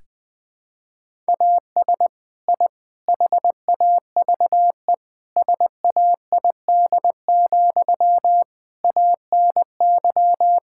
Morse code, heard as a keyed tone, about 20 wpm, 700 Hz. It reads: AS I HAVE SAID, ANY
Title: Sentences from Top 1000 Words - Review Part 6 - 20wpm